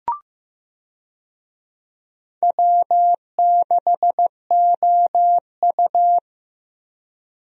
Morse code, heard as a keyed tone, about 15 wpm, 700 Hz. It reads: W6OU
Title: USA Call Signs - 4 Characters - Repeated 3x - 15wpm